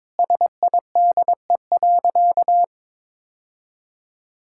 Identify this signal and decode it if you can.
Morse code, heard as a keyed tone, about 22 wpm, 700 Hz. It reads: SIDE.
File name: practice / 22wpm / Sentences with 2-Word Phrases - 22wpm